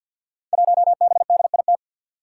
Morse code, heard as a keyed tone, about 46 wpm, 700 Hz. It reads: 1BDIT
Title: Five Challenging Letter-Number Combinations 50wpm